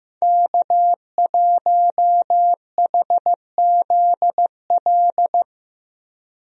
Morse code, heard as a keyed tone, about 15 wpm, 700 Hz. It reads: K1HZL
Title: USA Call Signs - 5 Characters - Repeated 3x - 15wpm